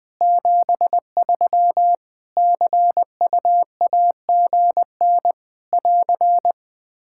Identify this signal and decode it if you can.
Morse code, heard as a keyed tone, about 20 wpm, 700 Hz. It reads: 73 CUAGN <AR>
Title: Practice QSO - Exchange 4 - Ending - 20wpm